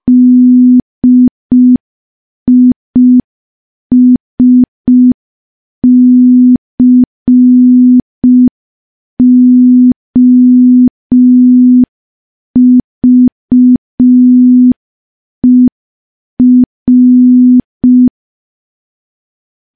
Morse code, heard as a keyed tone, about 5 wpm, 250 Hz. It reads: DISCOVER
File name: morse / morse-training-05WPM